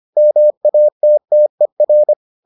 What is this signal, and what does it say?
Morse code, heard as a keyed tone, about 25 wpm, 600 Hz. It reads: MATTER